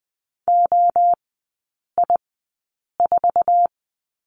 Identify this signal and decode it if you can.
Morse code, heard as a keyed tone, about 20 wpm, spaced as for 8 wpm, 700 Hz. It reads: OI4
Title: 047 Lesson 9e - Number 4 - Three Characters 20wpm